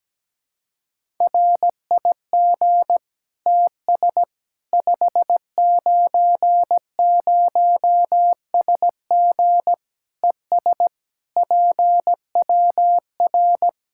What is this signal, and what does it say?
Morse code, heard as a keyed tone, about 17 wpm, 700 Hz. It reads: RIG TS 590SG ES PWR
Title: K1USN SST QSO Practice 17wpm